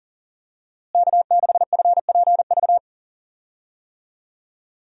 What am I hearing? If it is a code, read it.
Morse code, heard as a keyed tone, about 38 wpm, 700 Hz. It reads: K6FPV
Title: USA Call Signs - 5 Characters - Repeated 3x - 40wpm